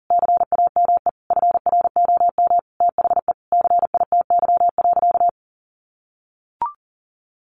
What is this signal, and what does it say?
Morse code, heard as a keyed tone, about 40 wpm, 700 Hz. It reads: CAME FROM THE CITY.
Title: Sentences from Top 1500 Words 40wpm